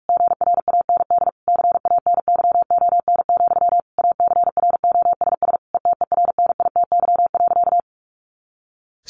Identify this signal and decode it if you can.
Morse code, heard as a keyed tone, about 44 wpm, 700 Hz. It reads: GRAND CANYON, ACROSS ETERNITY.
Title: A Sound Of Thunder 45wpm